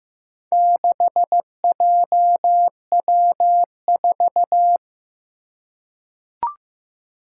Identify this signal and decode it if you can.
Morse code, heard as a keyed tone, about 15 wpm, 700 Hz. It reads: 6JW4
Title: Four Challenging Letter-Number Combinations 15wpm